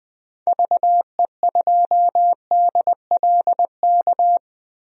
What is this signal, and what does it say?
Morse code, heard as a keyed tone, about 20 wpm, 700 Hz. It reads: VE2DLK
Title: DXCC-100 Call Signs - 6 Characters - 20wpm